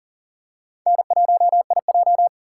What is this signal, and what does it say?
Morse code, heard as a keyed tone, about 40 wpm, 700 Hz. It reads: N1IJ
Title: USA Call Signs - 4 Characters - 3x Character Spacing - 40wpm